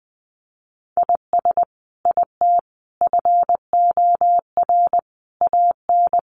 Morse code, heard as a keyed tone, about 20 wpm, 700 Hz. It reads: IS IT FOR AN
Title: Sentences from Top 1200 Words 20wpm